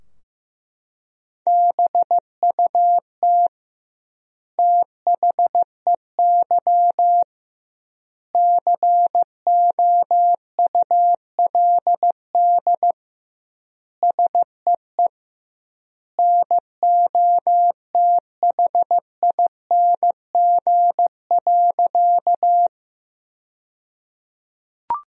Morse code, heard as a keyed tone, about 16 wpm, 700 Hz. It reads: BUT THEY COULD SEE NOTHING.
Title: 2x Word-Spacing - Sentences from Top 500 Words - 15wpm